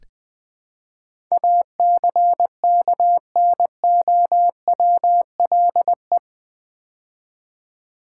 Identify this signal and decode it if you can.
Morse code, heard as a keyed tone, about 20 wpm, 700 Hz. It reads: ACKNOWLE
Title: Build A Word - 4 Letters At A Time - 11-Letter Words - 20wpm